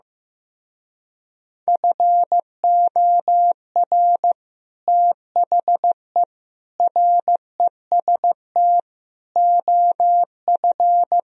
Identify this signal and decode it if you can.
Morse code, heard as a keyed tone, about 15 wpm, 700 Hz. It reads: FOR THE REST OF
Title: Speed-Racing - Sets of 4 Words 15wpm